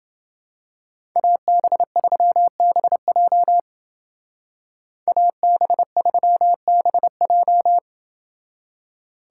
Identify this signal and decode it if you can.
Morse code, heard as a keyed tone, about 28 wpm, 700 Hz. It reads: AB3BJ AB3BJ